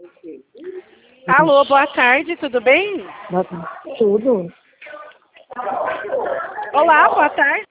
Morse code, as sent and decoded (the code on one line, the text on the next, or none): none